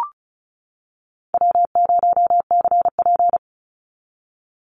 Morse code, heard as a keyed tone, about 34 wpm, 700 Hz. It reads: W0CP
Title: NA SOTA Activator Callsigns 35wpm